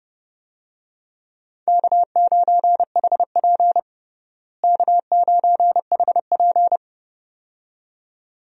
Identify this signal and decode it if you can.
Morse code, heard as a keyed tone, about 30 wpm, 700 Hz. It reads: K9HP K9HP
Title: USA Call Signs - 4 Characters - Repeated 2x - 30wpm